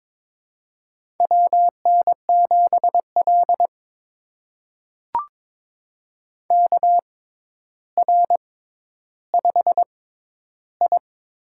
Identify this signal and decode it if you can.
Morse code, heard as a keyed tone, about 22 wpm, spaced as for 11 wpm, 700 Hz. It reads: WN7L KR5I
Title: USA Call Signs - 4 Characters - 6x Character Spacing - 22wpm